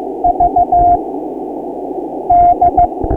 Morse code, heard as a keyed tone, about 15 wpm, 700 Hz, with noise. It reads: V D
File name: ARCHIV / Beacon 1999.65 / 2015 03 13small